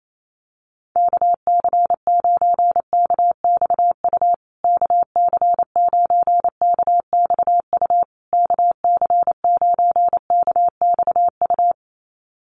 Morse code, heard as a keyed tone, about 28 wpm, 700 Hz. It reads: KC9KXU KC9KXU KC9KXU
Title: USA Call Signs Repeated 3x 28wpm